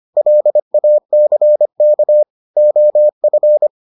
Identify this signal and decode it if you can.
Morse code, heard as a keyed tone, about 25 wpm, 600 Hz. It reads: LACK OF